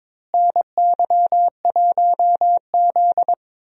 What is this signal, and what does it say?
Morse code, heard as a keyed tone, about 22 wpm, 700 Hz. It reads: NY1Z
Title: USA Call Signs - 4 Characters - Repeated 3x - 22wpm